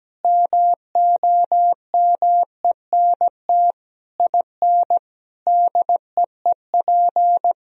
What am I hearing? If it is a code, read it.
Morse code, heard as a keyed tone, about 17 wpm, 700 Hz. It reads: MOMENT IN DEEP
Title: Sentences from Top 1000 Words - Review Part 3 - 17wpm